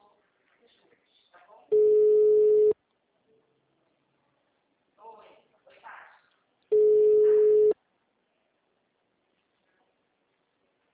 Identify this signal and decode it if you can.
Ringback tone (Europe, 425 Hz), with speech.